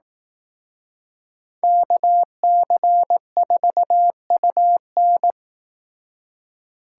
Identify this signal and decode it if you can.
Morse code, heard as a keyed tone, about 18 wpm, 700 Hz. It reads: KC4UN
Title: Speed Racing - USA 2x2 Call Signs 15wpm